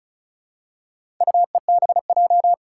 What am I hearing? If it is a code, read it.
Morse code, heard as a keyed tone, about 36 wpm, 690 Hz. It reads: UEBJ
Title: Four Challenging Letter-Number Combinations 35wpm